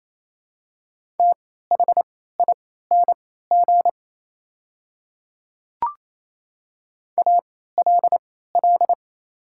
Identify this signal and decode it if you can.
Morse code, heard as a keyed tone, about 28 wpm, spaced as for 16 wpm, 700 Hz. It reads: THING ALL
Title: 3x Character Spacing - Top 100 Words - 28wpm